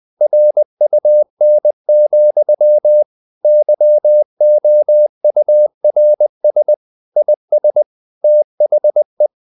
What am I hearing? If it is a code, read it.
Morse code, heard as a keyed tone, about 20 wpm, 600 Hz. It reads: RUN, YOURS IS THE
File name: CW / Practice / If20